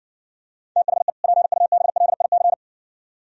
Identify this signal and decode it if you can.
Morse code, heard as a keyed tone, about 66 wpm, 700 Hz. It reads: THE PUBLIC